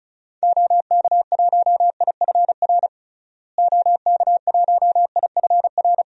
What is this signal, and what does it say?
Morse code, heard as a keyed tone, about 36 wpm, 700 Hz. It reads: OK1IFR OK1IFR
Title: DXCC-100 Call Signs - 6 Characters - Repeated 3x - 35wpm